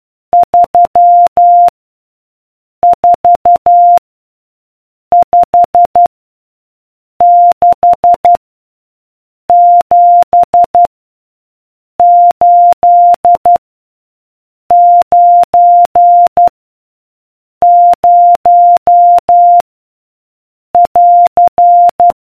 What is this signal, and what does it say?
Morse code, heard as a keyed tone, about 12 wpm, 700 Hz. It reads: 3 4 5 6 7 8 9 0 <AR>